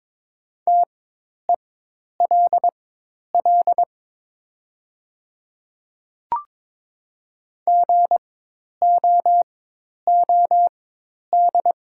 Morse code, heard as a keyed tone, about 22 wpm, spaced as for 10 wpm, 700 Hz. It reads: TELL GOOD
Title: 4x Character Spacing - Top 200 Words - 22wpm